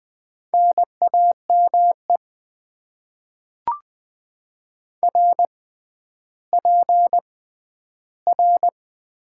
Morse code, heard as a keyed tone, about 20 wpm, spaced as for 11 wpm, 700 Hz. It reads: NAME RPR
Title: Top 90 QSO Elements - 6x Character Spacing - 20wpm